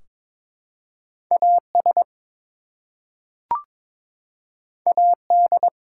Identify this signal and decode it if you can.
Morse code, heard as a keyed tone, about 22 wpm, 700 Hz. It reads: AS AD
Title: Two Letter Words - 22wpm